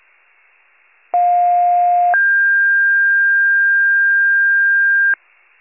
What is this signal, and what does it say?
Two-tone page, heard as a one-second tone, then a three-second tone, about 710 then 1600 Hz. It is heard from 1.1 to 5.1 s.